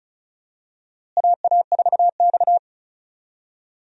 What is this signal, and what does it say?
Morse code, heard as a keyed tone, about 36 wpm, 700 Hz. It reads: AA4X
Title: USA Call Signs - 4 Characters - Repeated 3x - 35wpm